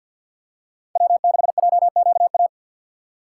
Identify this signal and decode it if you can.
Morse code, heard as a keyed tone, about 48 wpm, 690 Hz. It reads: W6JQA
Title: USA Call Signs - 5 Characters - 6x Character Spacing - 50wpm